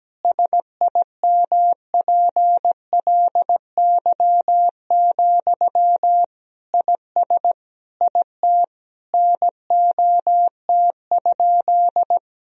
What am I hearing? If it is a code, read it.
Morse code, heard as a keyed tone, about 17 wpm, 700 Hz. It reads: SIMPLY, IS IT NOT?
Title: Sentences from Top 1000 Words - Review Part 4 - 17wpm